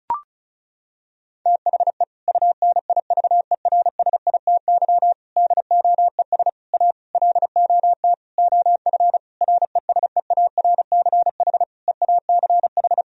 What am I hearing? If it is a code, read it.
Morse code, heard as a keyed tone, about 36 wpm, 700 Hz. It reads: THE UNIVERSITY DOES A LOT OF RESEARCH EACH
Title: Sentences from Top 1000 Words - Review Part 4 - 35wpm